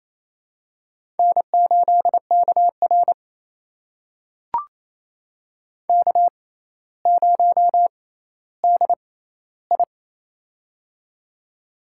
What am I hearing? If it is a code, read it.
Morse code, heard as a keyed tone, about 28 wpm, spaced as for 14 wpm, 700 Hz. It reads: N8KR K0DI
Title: USA Call Signs - 4 Characters - 6x Character Spacing - 28wpm